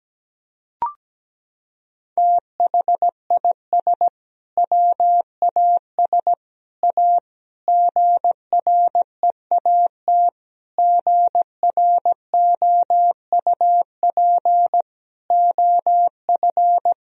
Morse code, heard as a keyed tone, about 17 wpm, 700 Hz. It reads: THIS WAS A GREAT GROUP OF
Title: Sentences from Top 1000 Words - Review Part 1 - 17wpm